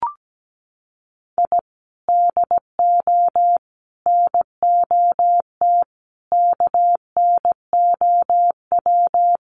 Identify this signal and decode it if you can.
Morse code, heard as a keyed tone, about 17 wpm, 700 Hz. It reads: I DO NOT KNOW
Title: Sets of 5 Words 17wpm